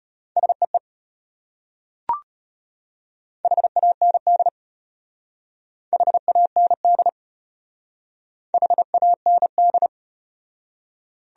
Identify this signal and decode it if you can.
Morse code, heard as a keyed tone, about 32 wpm, 700 Hz. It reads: SEE HAND HAND HAND